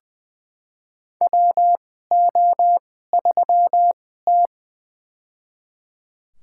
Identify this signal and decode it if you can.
Morse code, heard as a keyed tone, about 20 wpm, spaced as for 14 wpm, 700 Hz. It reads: WO3T
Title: USA Call Signs - 4 Characters - 2x Character Spacing - 20wpm